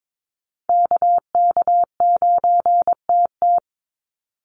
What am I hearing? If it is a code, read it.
Morse code, heard as a keyed tone, about 22 wpm, 700 Hz. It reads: KK9TT
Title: USA Call Signs 22wpm